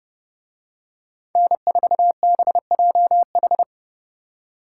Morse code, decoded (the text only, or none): N4BJH